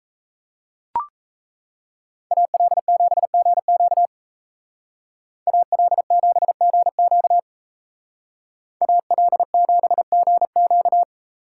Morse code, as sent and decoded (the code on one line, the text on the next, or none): .- .-.. --... --. --.- / .- .-.. --... --. --.- / .- .-.. --... --. --.-
AL7GQ AL7GQ AL7GQ